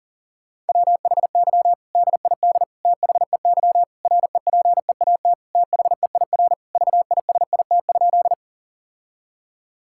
Morse code, heard as a keyed tone, about 37 wpm, 700 Hz. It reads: WHY DID THEY REPEAT THEIR VISIT?
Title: Sentences from Top 1600 Words 40wpm